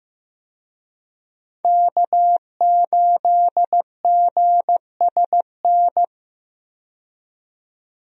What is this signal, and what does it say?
Morse code, heard as a keyed tone, about 15 wpm, 700 Hz. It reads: K8GSN
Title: USA Call Signs - 5 Characters - 15wpm